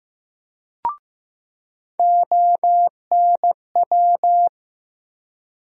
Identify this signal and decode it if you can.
Morse code, heard as a keyed tone, about 15 wpm, 700 Hz. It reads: ONW
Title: USA Call Sign Postfixes - 3 Letters 15wpm